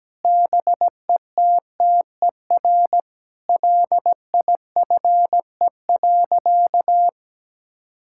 Morse code, heard as a keyed tone, about 17 wpm, 700 Hz. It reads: BETTER LIFE.
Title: Sentences from Top 1000 Words - Review Part 5 - 17wpm